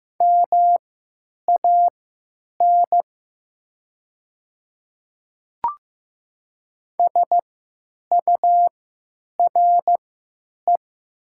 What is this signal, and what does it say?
Morse code, heard as a keyed tone, about 15 wpm, spaced as for 9 wpm, 700 Hz. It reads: MAN SURE
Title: 3x Character Spacing - Top 300 Words - 15wpm